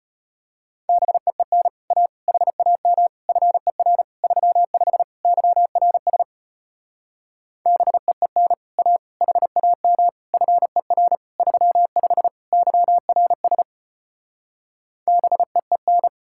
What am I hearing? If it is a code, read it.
Morse code, heard as a keyed tone, about 33 wpm, 700 Hz. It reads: BEEN A HAM FER 35 YRS BEEN A HAM FER 35 YRS BEEN